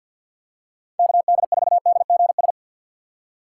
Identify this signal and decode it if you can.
Morse code, heard as a keyed tone, about 47 wpm, 700 Hz. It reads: KD4DGS
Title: USA Call Signs - 4x Character Spacing - 50wpm